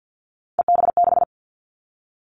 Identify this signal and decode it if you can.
Morse code, heard as a keyed tone, about 50 wpm, 700 Hz. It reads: EB6
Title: Callsign Prefixes for Top 100 DXCC - 50wpm